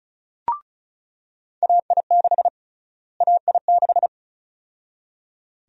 Morse code, heard as a keyed tone, about 36 wpm, 700 Hz. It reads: AI6 AI6
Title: USA Call Sign Prefixes - Repeated 2x - 35wpm